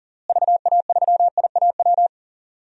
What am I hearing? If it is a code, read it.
Morse code, heard as a keyed tone, about 40 wpm, 700 Hz. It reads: VA3IAW
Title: DXCC-100 Call Signs - 6 Characters - Repeated 3x - 40wpm